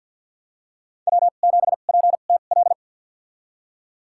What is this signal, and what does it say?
Morse code, heard as a keyed tone, about 52 wpm, spaced as for 36 wpm, 700 Hz. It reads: W7PTL
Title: USA Call Signs - 5 Characters - 2x Character Spacing - 50wpm